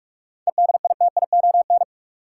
Morse code, heard as a keyed tone, about 42 wpm, 700 Hz. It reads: EDITION
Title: Sets of 5 Words - New Word Drill - 45wpm